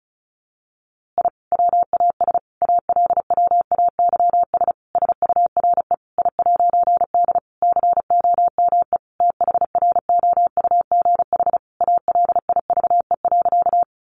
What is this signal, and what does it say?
Morse code, heard as a keyed tone, about 34 wpm, 700 Hz. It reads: I WAS ALWAYS SURE I'D COME THROUGH ALIVE.
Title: A Sound Of Thunder 35wpm